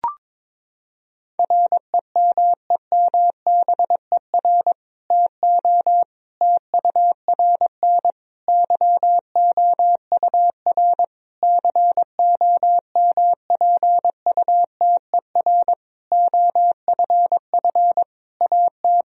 Morse code, heard as a keyed tone, about 22 wpm, 700 Hz. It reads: REMEMBER TO TURN YOUR COMPUTER OFF AT
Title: Sentences from Top 1000 Words - Review Part 3 - 22wpm